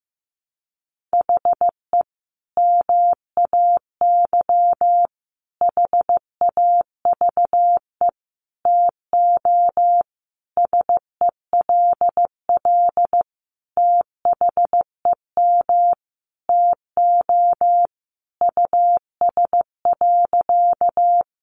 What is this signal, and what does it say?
Morse code, heard as a keyed tone, about 15 wpm, 700 Hz. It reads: HE MAY HAVE TO SELL THEM TO US.
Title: Sentences from Top 1100 Words 15wpm